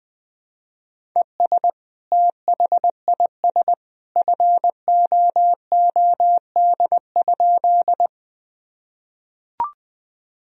Morse code, heard as a keyed tone, about 20 wpm, 700 Hz. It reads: ES THIS FOOD?
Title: Sentences from Top 300 Words and Abbreviations 20wpm